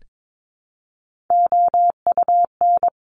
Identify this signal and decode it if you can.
Morse code, heard as a keyed tone, about 22 wpm, 700 Hz. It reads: OUN
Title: Top Three Letter Combinations in Words 22wpm